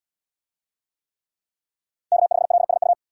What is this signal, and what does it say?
Morse code, heard as a keyed tone, about 72 wpm, 700 Hz. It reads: K5LSU